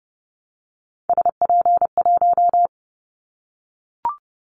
Morse code, heard as a keyed tone, about 30 wpm, 700 Hz. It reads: SP1
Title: Callsign Prefixes for Top 25 DXCC - 30wpm